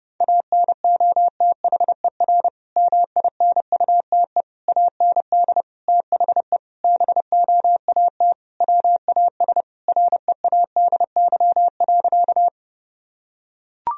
Morse code, heard as a keyed tone, about 30 wpm, 700 Hz. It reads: ANOTHER MINUTE AND THE BOAT WAS READY.
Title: Sentences from Top 1000 Words - Review Part 3 - 30wpm